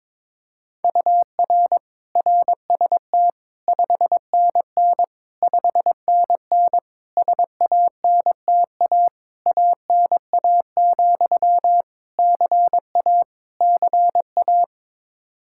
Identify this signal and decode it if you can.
Morse code, heard as a keyed tone, about 22 wpm, 700 Hz. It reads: UR RST 5NN 5NN SANTA ANA, CA CA
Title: Practice QSO - Exchange 1 - Beginning - 22wpm